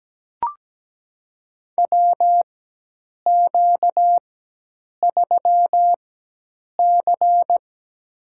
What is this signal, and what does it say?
Morse code, heard as a keyed tone, about 17 wpm, spaced as for 8 wpm, 700 Hz. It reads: WQ3C
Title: USA Call Signs - 4 Characters - 4x Character Spacing - 17wpm